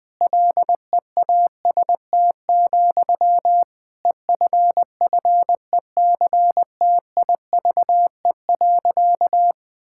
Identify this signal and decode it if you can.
Morse code, heard as a keyed tone, about 19 wpm, 700 Hz. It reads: LEAST, EFFECTIVE.
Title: Sentences from Top 1000 Words - Review Part 6 - 20wpm